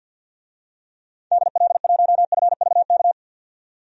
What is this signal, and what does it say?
Morse code, heard as a keyed tone, about 51 wpm, 700 Hz. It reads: DL1FVK
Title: DXCC-100 Call Signs - 6 Characters - 50wpm